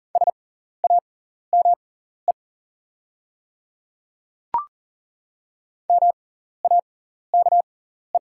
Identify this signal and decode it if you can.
Morse code, heard as a keyed tone, about 40 wpm, spaced as for 14 wpm, 690 Hz. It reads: SAME MAKE
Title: 6x Character Spacing - Top 200 Words - 40wpm